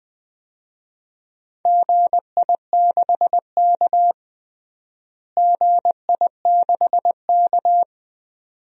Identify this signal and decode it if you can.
Morse code, heard as a keyed tone, about 20 wpm, 700 Hz. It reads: GI6K GI6K
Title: DXCC-100 Call Signs - 4 Characters - Repeated 2x - 20wpm